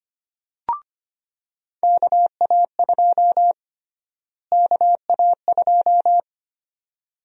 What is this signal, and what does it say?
Morse code, heard as a keyed tone, about 25 wpm, 700 Hz. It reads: KA2 KA2
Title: USA Call Sign Prefixes - Repeated 2x - 25wpm